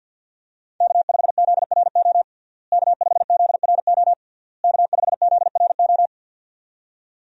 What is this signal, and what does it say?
Morse code, heard as a keyed tone, about 51 wpm, 700 Hz. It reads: K5ZRO K5ZRO K5ZRO
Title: USA Call Signs - 5 Characters - Repeated 3x - 50wpm